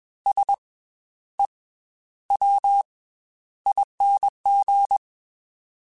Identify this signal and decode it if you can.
Morse code, heard as a keyed tone, about 21 wpm, 800 Hz. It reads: S E W ING
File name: Special INGs / Special INGs-mp3s / Slow ing @20